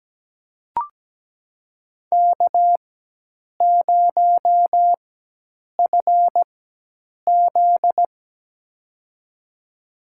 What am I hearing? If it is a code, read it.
Morse code, heard as a keyed tone, about 17 wpm, spaced as for 8 wpm, 700 Hz. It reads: K0FZ
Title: USA Call Signs - 4 Characters - 4x Character Spacing - 17wpm